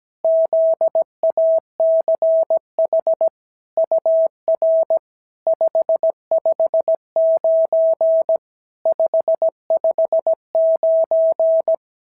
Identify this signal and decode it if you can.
Morse code, heard as a keyed tone, about 17 wpm, 650 Hz. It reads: ZACH UR 559 559
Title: SOTA QSO Practice 17wpm